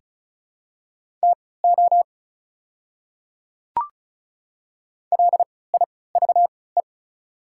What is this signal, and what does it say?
Morse code, heard as a keyed tone, about 34 wpm, spaced as for 20 wpm, 690 Hz. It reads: TO LIVE
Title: 3x Character Spacing - Top 200 Words - 35wpm